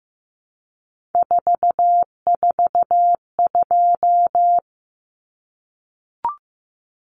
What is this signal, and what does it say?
Morse code, heard as a keyed tone, about 15 wpm, 700 Hz. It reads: 442
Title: Numbers - 3-Digit 15wpm